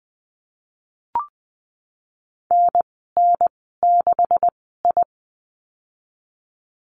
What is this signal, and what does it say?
Morse code, heard as a keyed tone, about 19 wpm, spaced as for 14 wpm, 700 Hz. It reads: NN6I
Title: 128 Lesson 21g - Number 6 - Call Signs 20wpm